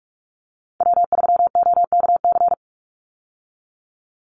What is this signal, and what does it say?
Morse code, heard as a keyed tone, about 44 wpm, 700 Hz. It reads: W3OKC
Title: Call Signs Repeated 2x - 45wpm